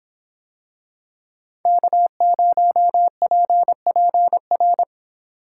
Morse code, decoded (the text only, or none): K0PPR